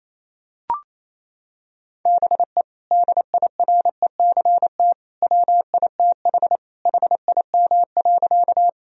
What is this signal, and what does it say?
Morse code, heard as a keyed tone, about 27 wpm, 700 Hz. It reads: BE DIRECT WITH HIM.
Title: Sentences from Top 1000 Words - Review Part 2 - 28wpm